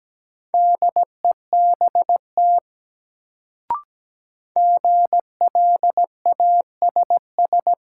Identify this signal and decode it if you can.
Morse code, heard as a keyed tone, about 17 wpm, 700 Hz. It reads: DEBT GLASS
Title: Sentences from Top 1300 Words - New Word Drill Alt - 17wpm